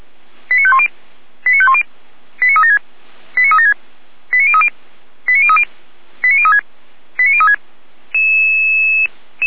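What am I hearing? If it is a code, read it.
Five-tone selective call (ZVEI), heard as quick runs of five tones, tones of about 70 ms. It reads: 89510 89510 89376 89376 89030 89030 89036 89036
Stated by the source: Five-tone ZVEI selective call received by SDR, from audio server of Feuerwehr Orlamünde (fire brigade), code 89140.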